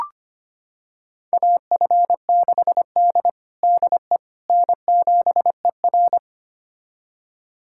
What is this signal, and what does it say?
Morse code, heard as a keyed tone, about 25 wpm, 700 Hz. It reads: AF6D DE N7ER
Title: USA Callsign DE Callsign - 4 Characters - 25wpm